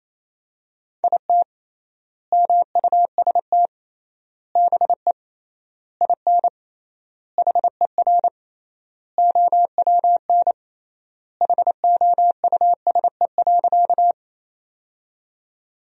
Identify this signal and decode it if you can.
Morse code, heard as a keyed tone, about 28 wpm, 700 Hz. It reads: IT MUST BE IN HER OWN HOUSE.
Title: 3x Word-Spacing - Sentences from Top 300 Words - 28wpm